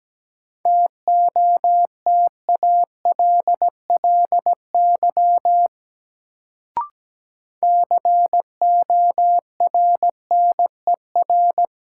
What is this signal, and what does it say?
Morse code, heard as a keyed tone, about 17 wpm, 700 Hz. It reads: TOTALLY CORNER
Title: Sentences from Top 1600 Words - New Word Drill - 17wpm